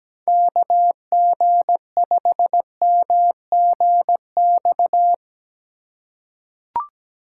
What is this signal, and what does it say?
Morse code, heard as a keyed tone, about 17 wpm, 700 Hz. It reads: KG5MGX
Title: USA Call Signs - 6 Characters - 3x Character Spacing - 17wpm